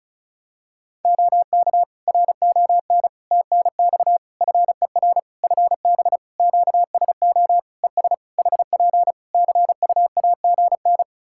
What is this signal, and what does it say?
Morse code, heard as a keyed tone, about 36 wpm, 700 Hz. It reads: OK RON TNX FER FB QSO ES HP CUAGN
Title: Practice QSO - Exchange 4 - Ending - 35wpm